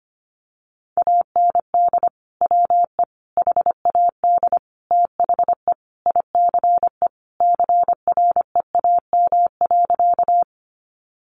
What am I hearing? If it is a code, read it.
Morse code, heard as a keyed tone, about 25 wpm, 700 Hz. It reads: AND WE HAD THE ICE CREAM.
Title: Sentences from Top 1500 Words 25wpm